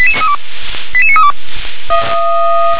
Five-tone selective call (ZVEI), heard as quick runs of five tones, tones of about 70 ms. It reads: 88932
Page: SDR/ZVEI